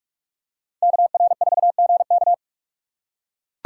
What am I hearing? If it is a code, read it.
Morse code, heard as a keyed tone, about 42 wpm, 700 Hz. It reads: KR4GK
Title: USA Call Signs - 5 Characters - 6x Character Spacing - 45wpm